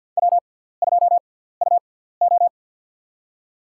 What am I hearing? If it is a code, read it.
Morse code, heard as a keyed tone, about 52 wpm, spaced as for 17 wpm, 700 Hz. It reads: W2UO
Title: USA Call Signs - 4 Characters - 6x Character Spacing - 50wpm